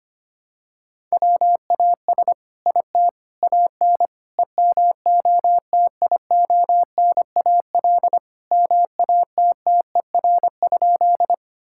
Morse code, heard as a keyed tone, about 25 wpm, 700 Hz. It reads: WAS IT AN EMOTIONAL MATTER?
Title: Sentences from Top 1000 Words - Review Part 5 - 25wpm